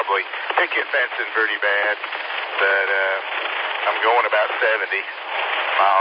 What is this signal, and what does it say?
Emergency Alert System (SAME) tones (header bursts), heard as harsh buzzing bursts, with speech. It is heard from 0.7 to 5.1 s.